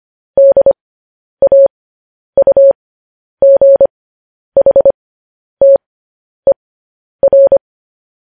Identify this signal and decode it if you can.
Morse code, heard as a keyed tone, about 25 wpm, spaced as for 10 wpm, 550 Hz. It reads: DAUGHTER
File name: podcast / News headlines-1x-25x10